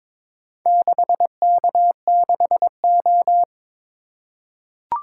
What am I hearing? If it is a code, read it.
Morse code, heard as a keyed tone, about 22 wpm, 700 Hz. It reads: 6K6O
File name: practice / 22wpm / Four Letter-Number Combinations - 22wpm